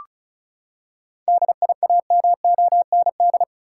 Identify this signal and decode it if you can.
Morse code, heard as a keyed tone, about 34 wpm, 700 Hz. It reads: DIAMOND
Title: Seven Letter Words - 35wpm